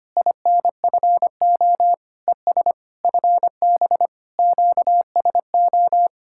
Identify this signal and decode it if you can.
Morse code, heard as a keyed tone, about 29 wpm, 700 Hz. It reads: INFO ES FB QSO
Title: Speed-Racing Common QSO Phrases - 25wpm